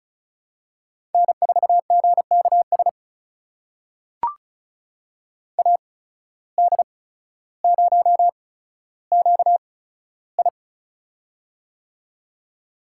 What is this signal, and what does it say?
Morse code, heard as a keyed tone, about 35 wpm, spaced as for 15 wpm, 690 Hz. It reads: N4GKS AD0QI
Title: USA Call Signs - 5 Characters - 8x Character Spacing - 35wpm